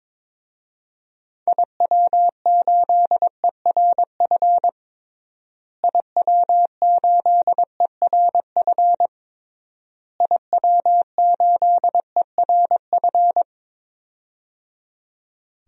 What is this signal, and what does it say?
Morse code, heard as a keyed tone, about 22 wpm, 700 Hz. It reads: IW8ERF IW8ERF IW8ERF
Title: DXCC-100 Call Signs - 6 Characters - Repeated 3x - 22wpm